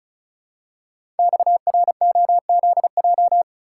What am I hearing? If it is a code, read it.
Morse code, heard as a keyed tone, about 36 wpm, 700 Hz. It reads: XROZJ
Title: Code Groups - 5 Letters - 35wpm